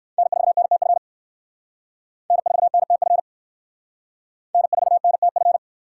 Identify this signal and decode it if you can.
Morse code, heard as a keyed tone, about 58 wpm, 700 Hz. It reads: N4NTF N4NTF N4NTF